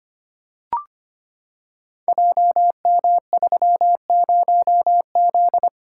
Morse code, heard as a keyed tone, about 25 wpm, 700 Hz. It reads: JM30Z
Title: Code Groups - 5 Letter:Numbers - 25wpm